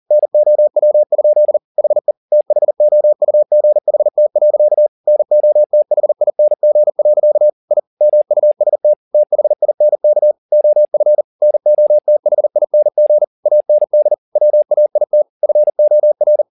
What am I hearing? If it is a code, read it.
Morse code, heard as a keyed tone, about 40 wpm, 600 Hz. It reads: NOW? HE THOUGHT. NOTHING. I MUST THINK OF NOTHING AND WAIT FOR
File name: the old man and the sea 2 40wpm